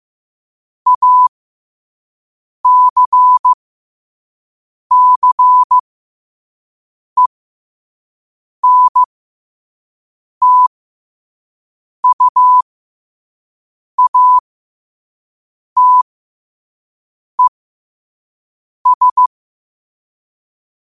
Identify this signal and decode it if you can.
Morse code, heard as a keyed tone, about 15 wpm, spaced as for 5 wpm, 1000 Hz. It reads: ACCENTUATES
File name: 5wpm